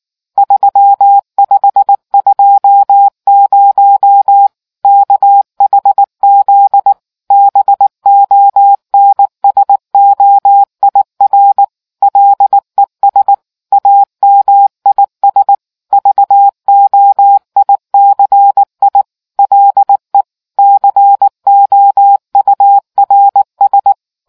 Morse code, heard as a keyed tone, about 19 wpm, 800 Hz. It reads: 3520 KHZ BONSOIR LES AMIS VOICI LE COURS